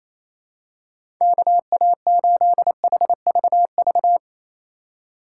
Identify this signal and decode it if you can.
Morse code, heard as a keyed tone, about 28 wpm, 700 Hz. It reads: KA8HVV
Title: USA Call Signs - 4x Character Spacing - 28wpm